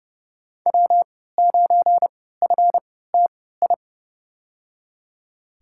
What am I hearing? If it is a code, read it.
Morse code, heard as a keyed tone, about 30 wpm, spaced as for 17 wpm, 700 Hz. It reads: W9FTI
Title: USA Call Signs - 5 Characters - 3x Character Spacing - 30wpm